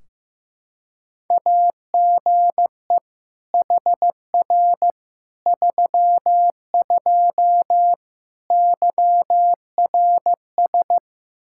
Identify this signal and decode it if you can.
Morse code, heard as a keyed tone, about 15 wpm, 700 Hz. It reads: AGE HR 32 YRS
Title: Practice QSO - Exchange 3 - More Info - 15wpm